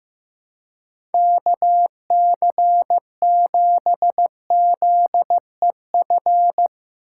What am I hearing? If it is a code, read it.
Morse code, heard as a keyed tone, about 15 wpm, 700 Hz. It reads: KC7ZEF
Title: USA Call Signs - 6 Characters - 4x Character Spacing - 15wpm